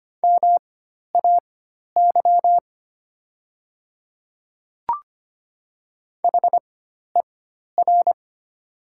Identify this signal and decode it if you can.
Morse code, heard as a keyed tone, about 25 wpm, spaced as for 12 wpm, 700 Hz. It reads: MAY HER
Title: 4x Character Spacing - Top 200 Words - 25wpm